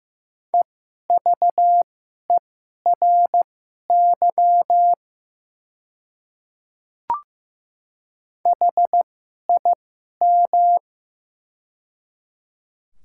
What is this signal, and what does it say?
Morse code, heard as a keyed tone, about 15 wpm, spaced as for 11 wpm, 700 Hz. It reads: EVERY HIM
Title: 2x Character Spacing - Top 200 Words - 15wpm